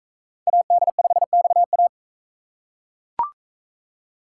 Morse code, heard as a keyed tone, about 40 wpm, 700 Hz. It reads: AD5XA